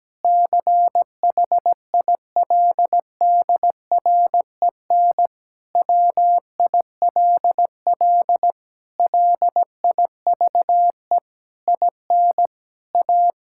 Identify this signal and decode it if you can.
Morse code, heard as a keyed tone, about 17 wpm, 700 Hz. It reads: CHILDREN WILL LIVE IN A
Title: Sentences from Top 1000 Words - Review Part 5 - 17wpm